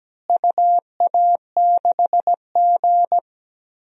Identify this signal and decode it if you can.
Morse code, heard as a keyed tone, about 17 wpm, 700 Hz. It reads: UA6G
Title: DXCC-100 Call Signs - 4 Characters - Repeated 3x - 17wpm